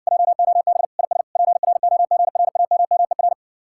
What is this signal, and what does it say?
Morse code, heard as a keyed tone, about 60 wpm, 700 Hz. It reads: JOB IS PROGRAMMER